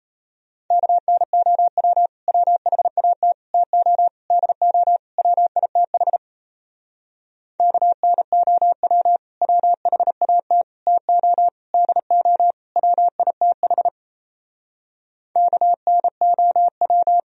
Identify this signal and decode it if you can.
Morse code, heard as a keyed tone, about 32 wpm, 700 Hz. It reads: KNOW WHAT TO DO WITH KNOW WHAT TO DO WITH KNOW
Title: Speed-Racing - Sets of 5 Words 28wpm